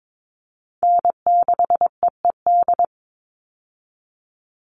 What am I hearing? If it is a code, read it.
Morse code, heard as a keyed tone, about 22 wpm, 700 Hz. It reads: N6EED
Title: Call Signs - 22wpm